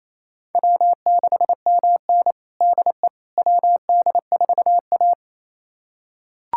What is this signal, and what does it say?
Morse code, heard as a keyed tone, about 28 wpm, 700 Hz. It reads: W6MN DE WD4A
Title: USA Callsign DE Callsign - 4 Characters - 28wpm